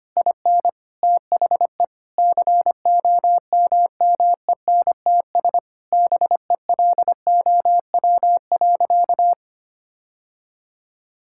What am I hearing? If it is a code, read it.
Morse code, heard as a keyed tone, about 25 wpm, 700 Hz. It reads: IN THE COMMENTS BELOW.
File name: practice / 25wpm / Sentences with 9-Word Phrases - 25wpm